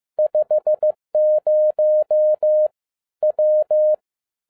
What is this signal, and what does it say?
Morse code, heard as a keyed tone, about 15 wpm, 610 Hz. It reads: 50 W
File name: 2011-10-mp3 / n12-JL3YMV 15wpm